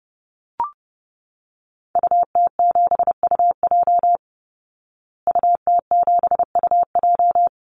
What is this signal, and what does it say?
Morse code, heard as a keyed tone, about 30 wpm, 700 Hz. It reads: UT7UJ UT7UJ
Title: Call Signs Repeated 2x - 30wpm